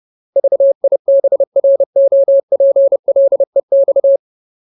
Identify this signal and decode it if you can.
Morse code, heard as a keyed tone, about 30 wpm, 550 Hz. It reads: VIBROPLEX
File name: Ultimate Word List / 30WPM Ultimate